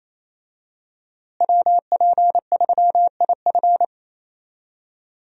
Morse code, heard as a keyed tone, about 27 wpm, 700 Hz. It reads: WP3IF
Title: USA Call Signs - 5 Characters - Repeated 2x - 28wpm